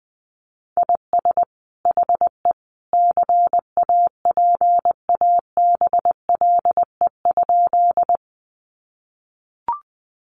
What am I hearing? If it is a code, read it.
Morse code, heard as a keyed tone, about 20 wpm, 700 Hz. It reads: IS HE CAPABLE?
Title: Sentences from Top 1200 Words 20wpm